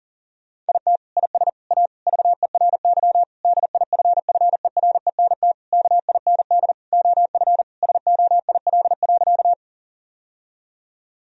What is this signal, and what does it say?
Morse code, heard as a keyed tone, about 40 wpm, 700 Hz. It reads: IT IS A VERY DIFFERENT KIND OF SOIL.
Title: Sentences from Top 1000 Words - Review Part 4 - 40wpm